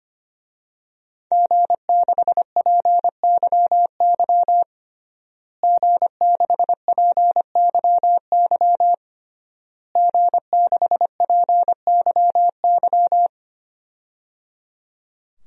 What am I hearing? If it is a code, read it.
Morse code, heard as a keyed tone, about 25 wpm, 700 Hz. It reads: G6PYY G6PYY G6PYY
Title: DXCC-100 Call Signs - 5 Characters - Repeated 3x - 25wpm